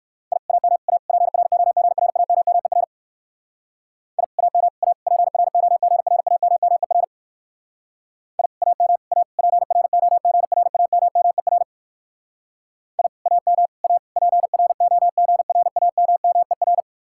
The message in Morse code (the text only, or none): I AM A PROGRAMMER I AM A PROGRAMMER I AM A PROGRAMMER I AM A PROGRAMMER